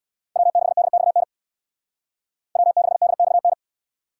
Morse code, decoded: W6DXN W6DXN

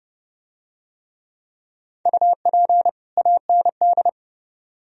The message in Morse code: UP AND

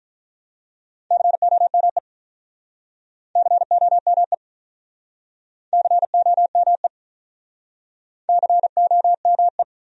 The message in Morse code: COME COME COME COME

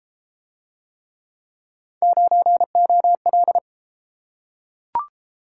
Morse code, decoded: 9OL

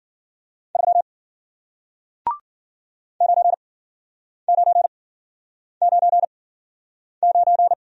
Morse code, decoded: 3 9999

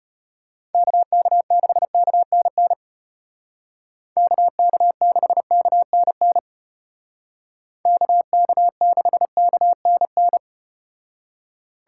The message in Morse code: KK6KNN KK6KNN KK6KNN

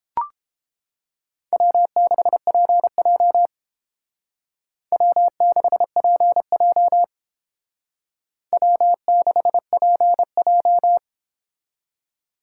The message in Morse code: W6PJ W6PJ W6PJ

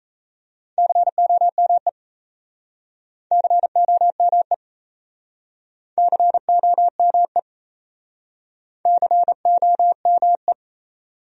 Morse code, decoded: COME COME COME COME

